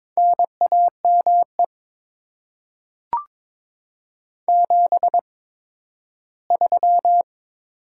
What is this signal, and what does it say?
Morse code, heard as a keyed tone, about 22 wpm, spaced as for 13 wpm, 700 Hz. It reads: NAME 73
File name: Top 20 QSO Elements - 8x Character Spacing - 22wpm